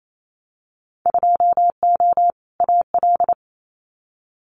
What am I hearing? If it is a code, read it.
Morse code, heard as a keyed tone, about 28 wpm, 700 Hz. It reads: 2O AL
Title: Winter Field Day Exchanges Encore 28wpm